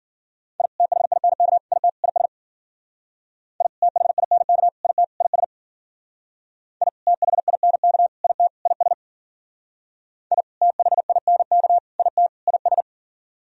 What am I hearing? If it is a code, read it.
Morse code, heard as a keyed tone, about 54 wpm, 700 Hz. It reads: I THINK IT IS I THINK IT IS I THINK IT IS I THINK IT IS